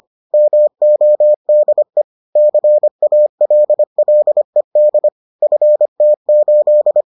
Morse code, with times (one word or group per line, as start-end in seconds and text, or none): 0.3-2.0 MODE
2.4-5.1 CALLED
5.4-7.0 FT8